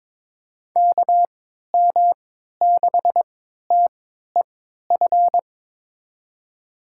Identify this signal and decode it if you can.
Morse code, heard as a keyed tone, about 21 wpm, spaced as for 12 wpm, 700 Hz. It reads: KM6TEF
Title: USA Call Signs - 6 Characters - 3x Character Spacing - 22wpm